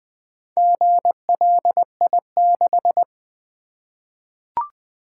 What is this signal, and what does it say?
Morse code, heard as a keyed tone, about 20 wpm, 700 Hz. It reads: GLI6
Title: Four Letter-Number Combinations - 20wpm